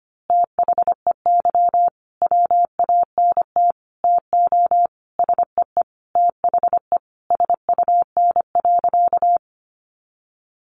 Morse code, heard as a keyed tone, about 25 wpm, 700 Hz. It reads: THEY WANT TO SEE THE SUN.
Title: Easy Phrases 25wpm